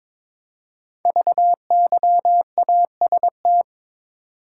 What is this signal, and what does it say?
Morse code, heard as a keyed tone, about 23 wpm, 700 Hz. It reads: VYAST
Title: Code Groups - 5 Letters - 22wpm